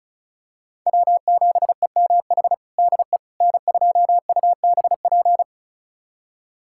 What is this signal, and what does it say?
Morse code, heard as a keyed tone, about 36 wpm, 700 Hz. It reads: W7EMH DE N2UBP
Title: USA Callsign DE Callsign - 5 Characters - 35wpm